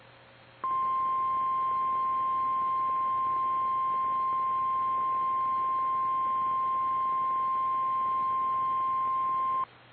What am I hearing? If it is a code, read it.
Emergency Alert System (SAME) tones (the 1050 Hz attention tone). It is heard from 0.6 to 9.7 s.
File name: eas / KKPRKXPN / KKPRKXPN